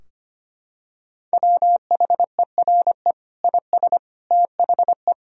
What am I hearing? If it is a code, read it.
Morse code, heard as a keyed tone, about 25 wpm, 700 Hz. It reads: WHERE IS THE
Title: Sentences from Top 1000 Words - Review Part 2 - 25wpm